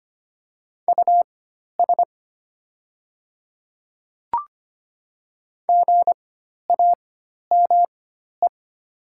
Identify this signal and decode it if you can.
Morse code, heard as a keyed tone, about 25 wpm, spaced as for 12 wpm, 700 Hz. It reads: US GAME
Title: 4x Character Spacing - Top 500 Words - 25wpm